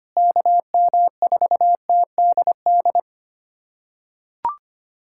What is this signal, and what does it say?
Morse code, heard as a keyed tone, about 25 wpm, 700 Hz. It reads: KM4TDD
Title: USA Callsign DE Callsign - 6 Characters - 25wpm